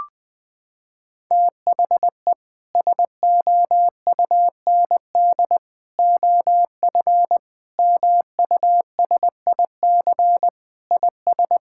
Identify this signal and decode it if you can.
Morse code, heard as a keyed tone, about 20 wpm, 700 Hz. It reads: THE SOUND OF MUSIC IS
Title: Sentences with 3-Word Phrases - 20wpm